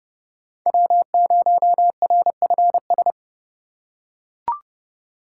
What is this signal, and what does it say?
Morse code, heard as a keyed tone, about 30 wpm, 700 Hz. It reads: W0RFS
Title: USA Call Signs - 2x Character Spacing - 30wpm